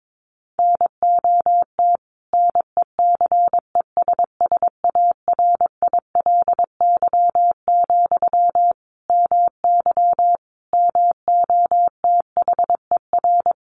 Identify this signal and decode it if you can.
Morse code, heard as a keyed tone, about 22 wpm, 700 Hz. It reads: NOT NECESSARILY, MY MOTHER
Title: Sentences from Top 1400 Words 22wpm